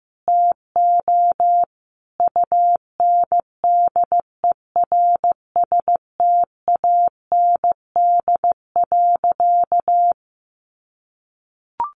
Morse code, heard as a keyed tone, about 15 wpm, 700 Hz. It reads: TO UNDERSTAND.
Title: Sentences from Top 1100 Words 15wpm